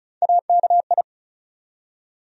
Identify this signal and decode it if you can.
Morse code, heard as a keyed tone, about 34 wpm, 700 Hz. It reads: AKI
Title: USA Call Sign Postfixes - 3 Letters 35wpm